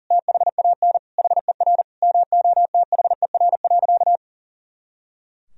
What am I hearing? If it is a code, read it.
Morse code, heard as a keyed tone, about 37 wpm, 700 Hz. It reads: THAN HER MOTHER.
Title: Sentences from Top 400 Words and Abbreviations 40wpm